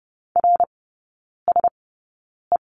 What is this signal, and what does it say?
Morse code, heard as a keyed tone, about 30 wpm, spaced as for 9 wpm, 700 Hz. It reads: RSE
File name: 081 Lesson 14e - Number 2 - Three Characters 30wpm